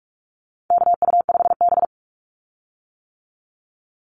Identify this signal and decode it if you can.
Morse code, heard as a keyed tone, about 44 wpm, 700 Hz. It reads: KU5B
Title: Call Signs - 45wpm